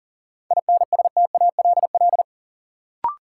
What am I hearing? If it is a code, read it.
Morse code, heard as a keyed tone, about 40 wpm, 700 Hz. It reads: INSTALL
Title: Sentences from Top 1600 Words - New Word Drill - 40wpm